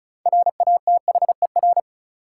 Morse code, heard as a keyed tone, about 36 wpm, 700 Hz. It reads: RATHER
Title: Sets of 2 Words - New Word Drill - 35wpm